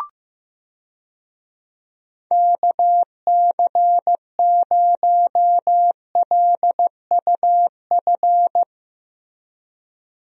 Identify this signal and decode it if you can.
Morse code, heard as a keyed tone, about 16 wpm, 700 Hz. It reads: KC0LUF
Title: USA Call Signs - 6 Characters - Repeated 2x - 15wpm